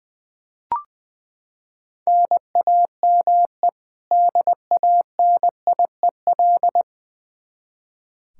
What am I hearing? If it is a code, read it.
Morse code, heard as a keyed tone, about 20 wpm, 700 Hz. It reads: NAME DANIEL
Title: Practice QSO - Exchange 1 - Beginning - 20wpm